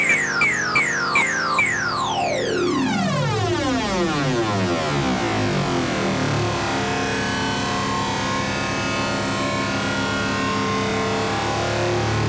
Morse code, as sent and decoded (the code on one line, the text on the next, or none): none